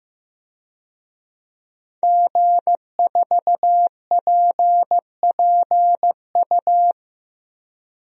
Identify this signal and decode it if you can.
Morse code, heard as a keyed tone, about 15 wpm, 700 Hz. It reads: G4PPU
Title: DXCC-100 Call Signs - 5 Characters - Repeated 2x - 15wpm